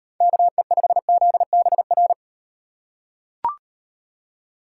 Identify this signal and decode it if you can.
Morse code, heard as a keyed tone, about 38 wpm, 700 Hz. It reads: KE5ZBR